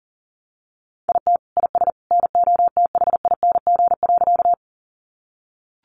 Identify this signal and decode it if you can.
Morse code, heard as a keyed tone, about 40 wpm, 700 Hz. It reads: IT IS NOTHING.
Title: Sentences from Top 500 Words 40wpm